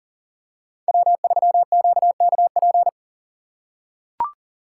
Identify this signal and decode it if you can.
Morse code, heard as a keyed tone, about 40 wpm, 700 Hz. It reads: W3QKP
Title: USA Call Signs - 5 Characters - 3x Character Spacing - 40wpm